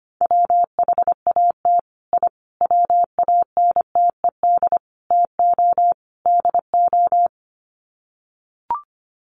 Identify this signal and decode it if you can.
Morse code, heard as a keyed tone, about 25 wpm, 700 Hz. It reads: WHAT I WANTED TO DO
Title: Sets of 5 Words Encore2 25wpm